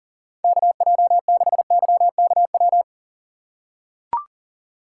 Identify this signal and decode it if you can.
Morse code, heard as a keyed tone, about 38 wpm, 700 Hz. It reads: KJ6YKW
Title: USA Call Signs - 6 Characters - 6x Character Spacing - 40wpm